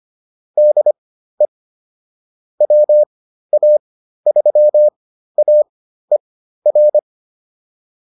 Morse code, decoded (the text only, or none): DE WA3AER